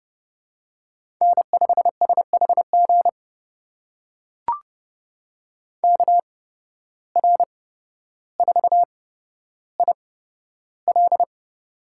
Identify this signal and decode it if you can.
Morse code, heard as a keyed tone, about 31 wpm, spaced as for 13 wpm, 700 Hz. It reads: N5SHG KR4IL
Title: USA Call Signs - 5 Characters - 8x Character Spacing - 30wpm